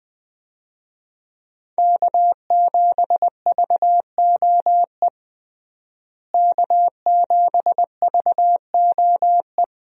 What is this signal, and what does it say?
Morse code, heard as a keyed tone, about 20 wpm, 700 Hz. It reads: K7VOE K7VOE
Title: USA Call Signs - 5 Characters - Repeated 2x - 20wpm